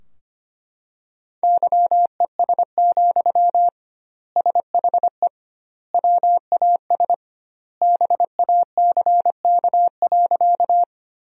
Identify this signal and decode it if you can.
Morse code, heard as a keyed tone, about 25 wpm, 700 Hz. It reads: YES, SHE WAS BACK.
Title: 2x Word-Spacing - Sentences from Top 500 Words - 25wpm